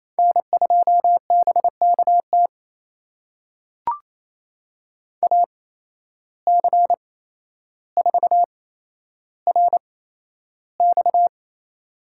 Morse code, heard as a keyed tone, about 28 wpm, spaced as for 12 wpm, 700 Hz. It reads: N2BKT AC4RX
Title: USA Call Signs - 5 Characters - 8x Character Spacing - 28wpm